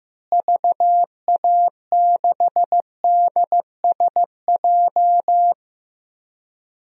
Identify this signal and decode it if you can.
Morse code, heard as a keyed tone, about 16 wpm, 700 Hz. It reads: VA6DSJ
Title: DXCC-100 Call Signs - 6 Characters - Repeated 2x - 15wpm